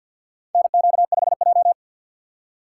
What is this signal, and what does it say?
Morse code, heard as a keyed tone, about 53 wpm, 700 Hz. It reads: NQ5J
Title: USA Call Signs - 4 Characters - 3x Character Spacing - 50wpm